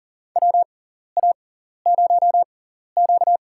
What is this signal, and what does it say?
Morse code, heard as a keyed tone, about 40 wpm, spaced as for 14 wpm, 700 Hz. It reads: WA0Q
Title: USA Call Signs - 4 Characters - 6x Character Spacing - 40wpm